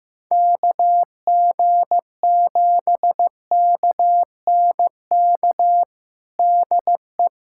Morse code, heard as a keyed tone, about 15 wpm, 700 Hz. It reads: KG7KNK DE
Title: USA Callsign DE Callsign - 6 Characters - 15wpm